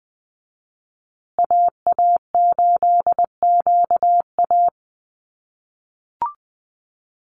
Morse code, heard as a keyed tone, about 20 wpm, 700 Hz. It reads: AA8QA
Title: USA Call Signs Repeated 3x 20wpm